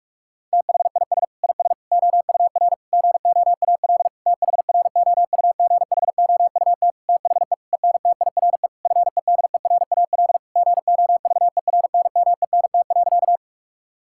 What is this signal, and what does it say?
Morse code, heard as a keyed tone, about 41 wpm, 700 Hz. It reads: THIS IS OUR GOAL THROUGHOUT THE ENTIRE FEDERAL GOVERNMENT.
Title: Sentences from Top 1000 Words - Review Part 5 - 45wpm